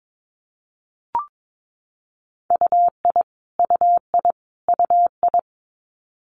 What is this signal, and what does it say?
Morse code, heard as a keyed tone, about 22 wpm, 700 Hz. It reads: UI UI UI
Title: Multi-letter Phonemes Repeated 3x 22wpm